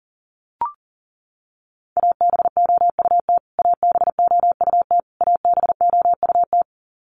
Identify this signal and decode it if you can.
Morse code, heard as a keyed tone, about 40 wpm, 700 Hz. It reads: ABOUT ABOUT ABOUT
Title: Top 500 Words Repeated 40wpm